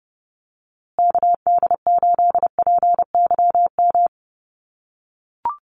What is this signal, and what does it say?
Morse code, heard as a keyed tone, about 30 wpm, 700 Hz. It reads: KD8PYM
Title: USA Call Signs Repeated 2x 30wpm